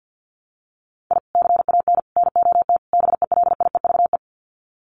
Morse code, heard as a keyed tone, about 50 wpm, 700 Hz. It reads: I CAN NOT BELIEVE
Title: Sets of 4 Words 50wpm